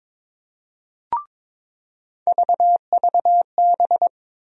Morse code, heard as a keyed tone, about 22 wpm, 700 Hz. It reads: VVB
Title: USA Call Sign Postfixes - 3 Letters 22wpm